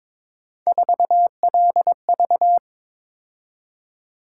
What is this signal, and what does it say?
Morse code, heard as a keyed tone, about 21 wpm, 700 Hz. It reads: 4LV
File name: practice / 22wpm / Three Letter-Number Combinations - 22wpm